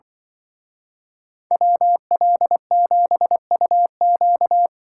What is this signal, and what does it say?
Morse code, heard as a keyed tone, about 25 wpm, 700 Hz. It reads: WL7UQ
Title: Speed Racing - USA 2x2 Call Signs 20wpm